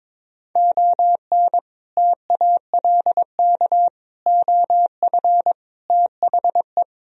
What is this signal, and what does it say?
Morse code, heard as a keyed tone, about 23 wpm, 700 Hz. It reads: ON TALK OF THE
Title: Speed-Racing - Sets of 5 Words 22wpm